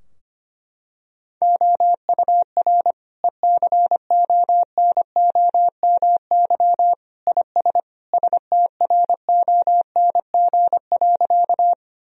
Morse code, heard as a keyed tone, about 25 wpm, 700 Hz. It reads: OUR ECONOMY IS STRONG.
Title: Sentences from Top 1000 Words - Review Part 3 - 25wpm